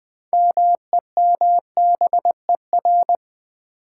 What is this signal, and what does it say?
Morse code, heard as a keyed tone, about 21 wpm, 700 Hz. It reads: MEMBER
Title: Sets of 3 Words - New Word Drill - 20wpm